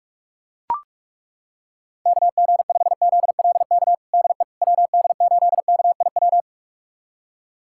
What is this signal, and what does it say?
Morse code, heard as a keyed tone, about 42 wpm, 700 Hz. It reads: KG5ZLK DE WD8KIW
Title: USA Callsign DE Callsign - 6 Characters - 45wpm